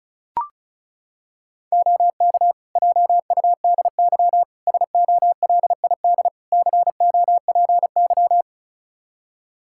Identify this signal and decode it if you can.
Morse code, heard as a keyed tone, about 36 wpm, 700 Hz. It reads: OK JUDY SOLID COPY
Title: Practice QSO - Exchange 3 - More Info - 35wpm